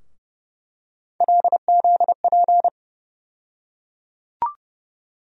Morse code, heard as a keyed tone, about 30 wpm, 700 Hz. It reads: LZP
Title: USA Call Sign Postfixes - 3 Letters 30wpm